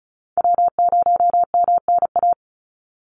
Morse code, heard as a keyed tone, about 34 wpm, 700 Hz. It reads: W0MNA
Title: NA SOTA Chaser Callsigns 35wpm